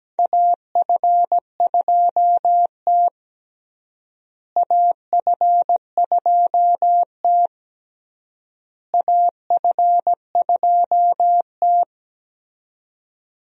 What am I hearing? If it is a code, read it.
Morse code, heard as a keyed tone, about 17 wpm, 700 Hz. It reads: AF2T AF2T AF2T
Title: USA Call Signs - 4 Characters - Repeated 3x - 17wpm